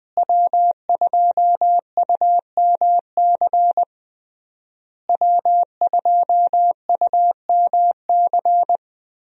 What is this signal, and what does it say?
Morse code, heard as a keyed tone, about 20 wpm, 700 Hz. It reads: W2UMC W2UMC
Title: USA Call Signs - 5 Characters - Repeated 3x - 20wpm